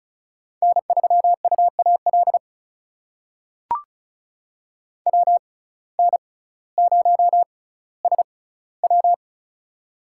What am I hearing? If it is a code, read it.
Morse code, heard as a keyed tone, about 36 wpm, spaced as for 18 wpm, 700 Hz. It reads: N3UAL WN0SW
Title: USA Call Signs - 5 Characters - 6x Character Spacing - 35wpm